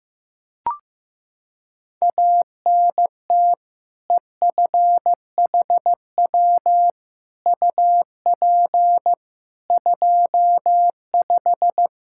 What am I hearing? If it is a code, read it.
Morse code, heard as a keyed tone, about 15 wpm, 700 Hz. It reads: ANT EFHW UP 25
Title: K1USN SST QSO Practice 15wpm